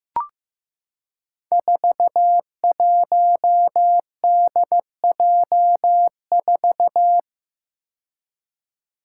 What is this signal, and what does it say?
Morse code, heard as a keyed tone, about 15 wpm, 700 Hz. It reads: 41DJ4
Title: Five Challenging Letter-Number Combinations 15wpm